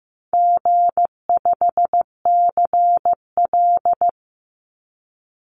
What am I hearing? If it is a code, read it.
Morse code, heard as a keyed tone, about 15 wpm, 700 Hz. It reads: G5CL
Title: Easy Call Signs Repeated 2x - 15wpm